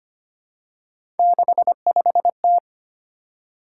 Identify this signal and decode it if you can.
Morse code, heard as a keyed tone, about 25 wpm, 700 Hz. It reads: 65T
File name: Three Letter-Number Combinations - 25wpm